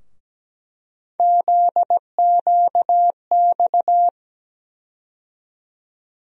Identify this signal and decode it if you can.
Morse code, heard as a keyed tone, about 17 wpm, 700 Hz. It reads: ZQX
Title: USA Call Sign Postfixes - 3 Letters 17wpm